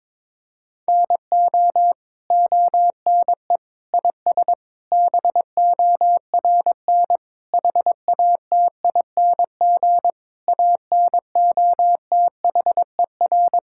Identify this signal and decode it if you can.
Morse code, heard as a keyed tone, about 22 wpm, 700 Hz. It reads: NO ONE IS BORN HATING ANOTHER
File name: Sentences with 9-Word Phrases - 22wpm